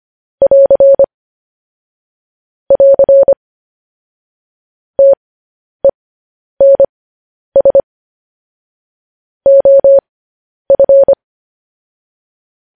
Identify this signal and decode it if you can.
Morse code, heard as a keyed tone, about 25 wpm, spaced as for 10 wpm, 550 Hz. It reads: <AR> <AR> TENS OF
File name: podcast / News headlines-1x-25x10